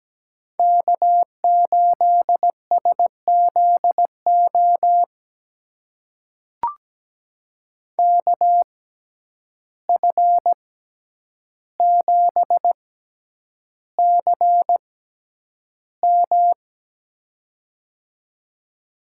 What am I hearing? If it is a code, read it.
Morse code, heard as a keyed tone, about 17 wpm, spaced as for 9 wpm, 700 Hz. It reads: K8SZO KF7CM
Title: USA Call Signs - 5 Characters - 6x Character Spacing - 17wpm